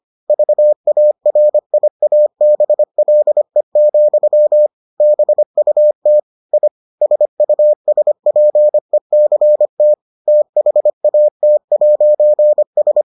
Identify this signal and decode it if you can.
Morse code, heard as a keyed tone, about 25 wpm, 600 Hz. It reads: VARIABLE, BUT I SUSPECT THAT'S